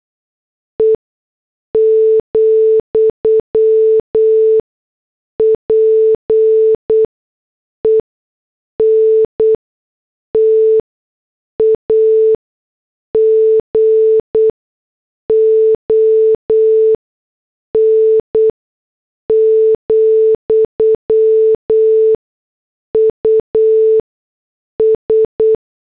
Morse code, heard as a keyed tone, about 8 wpm, spaced as for 6 wpm, 440 Hz. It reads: E, PENTAGON, US